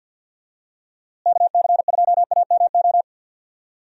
Morse code, heard as a keyed tone, about 50 wpm, 700 Hz. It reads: KC2AMO
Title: USA Call Signs - 6 Characters - Repeated 2x - 50wpm